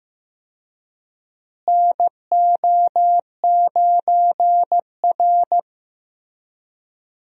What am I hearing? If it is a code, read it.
Morse code, heard as a keyed tone, about 15 wpm, 700 Hz. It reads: NO9R
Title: USA Call Signs - 4 Characters - Repeated 2x - 15wpm